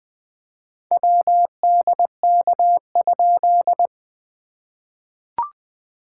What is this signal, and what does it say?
Morse code, heard as a keyed tone, about 20 wpm, 700 Hz. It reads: WDK?
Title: POTA QSO Practice - 20wpm